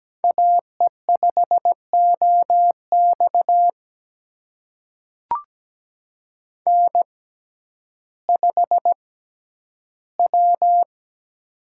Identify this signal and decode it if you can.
Morse code, heard as a keyed tone, about 17 wpm, spaced as for 10 wpm, 700 Hz. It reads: AE5OX N5W
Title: USA Call Signs - 5 Characters - 6x Character Spacing - 17wpm